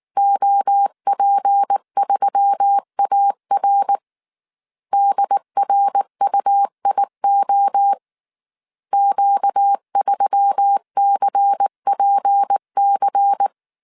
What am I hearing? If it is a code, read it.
Morse code, heard as a keyed tone, about 19 wpm, 800 Hz. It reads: OP3AR DRUIO Q3CPC
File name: CW / DET / 1 6 1500C S C 18wpm 2356